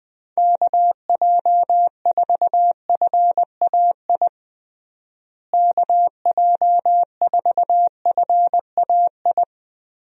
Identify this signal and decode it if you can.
Morse code, heard as a keyed tone, about 20 wpm, 700 Hz. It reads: KJ4FAI KJ4FAI
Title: USA Call Signs - 6 Characters - Repeated 3x - 20wpm